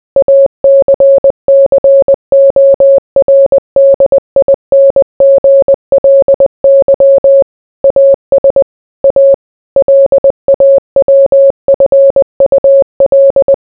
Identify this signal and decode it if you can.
Morse code, heard as a keyed tone, about 20 wpm, 550 Hz. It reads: ACCORDINGLY AS A LAWFUL